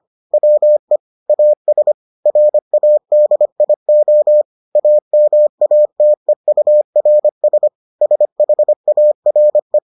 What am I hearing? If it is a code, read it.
Morse code, heard as a keyed tone, about 25 wpm, 600 Hz. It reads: WE AS RADIO AMATEURS SHARE